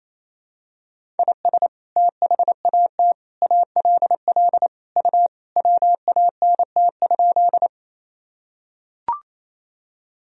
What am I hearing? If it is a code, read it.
Morse code, heard as a keyed tone, about 28 wpm, 700 Hz. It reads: IS THAT ALL U WANT?
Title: Sentences from Top 200 Words and Abbreviations 28wpm